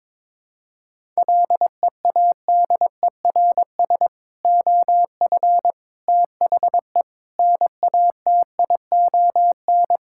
Morse code, heard as a keyed tone, about 22 wpm, 700 Hz. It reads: LEADERS OF THE NATION
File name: practice / 22wpm / Sentences with 3-Word Phrases - 22wpm